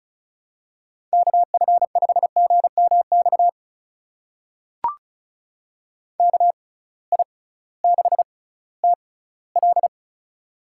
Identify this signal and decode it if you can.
Morse code, heard as a keyed tone, about 36 wpm, spaced as for 19 wpm, 700 Hz. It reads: KF5GMX KI6TL
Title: USA Call Signs - 6x Character Spacing - 35wpm